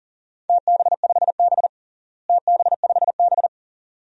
Morse code, heard as a keyed tone, about 40 wpm, 700 Hz. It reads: TB5B TB5B
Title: DXCC-100 Call Signs - 4 Characters - Repeated 2x - 40wpm